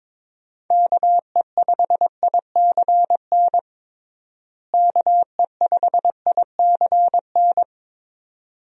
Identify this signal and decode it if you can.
Morse code, heard as a keyed tone, about 22 wpm, 700 Hz. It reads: KE5ICN KE5ICN
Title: USA Call Signs - 6 Characters - Repeated 2x - 22wpm